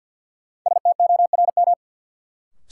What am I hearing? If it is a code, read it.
Morse code, heard as a keyed tone, about 52 wpm, 700 Hz. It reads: STORM
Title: Sentences from Top 1500 Words - New Word Drill Alt - 50wpm